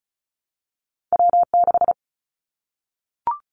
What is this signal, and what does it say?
Morse code, heard as a keyed tone, about 34 wpm, 700 Hz. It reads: W6
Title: Callsign Prefixes for Top 25 DXCC - Repeated 3x - 35wpm